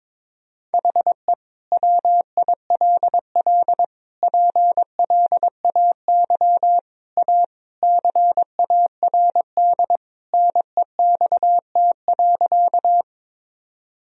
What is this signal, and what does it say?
Morse code, heard as a keyed tone, about 22 wpm, 700 Hz. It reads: HE WILL PLAY A CARD NEXT.
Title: Sentences from Top 1000 Words - Review Part 4 - 22wpm